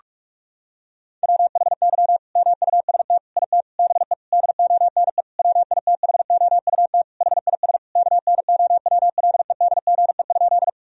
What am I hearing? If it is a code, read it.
Morse code, heard as a keyed tone, about 46 wpm, 700 Hz. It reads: WHY MUST IT BE DONE WITHOUT HIS KNOWLEDGE?
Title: Sentences from Top 1000 Words - Review Part 3 - 45wpm